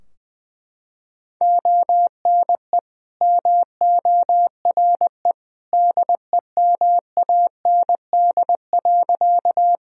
Morse code, heard as a keyed tone, about 20 wpm, 700 Hz. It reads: ONE MORE DEMAND.
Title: Sentences from Top 1000 Words - Review Part 4 - 20wpm